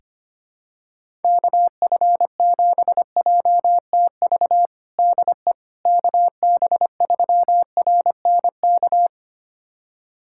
Morse code, heard as a keyed tone, about 25 wpm, 700 Hz. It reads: KF7JTV DE KB3RNK
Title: USA Callsign DE Callsign 25wpm